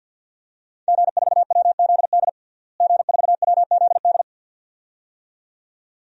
Morse code, heard as a keyed tone, about 51 wpm, 700 Hz. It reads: G4WZD G4WZD
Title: DXCC-100 Call Signs - 5 Characters - Repeated 2x - 50wpm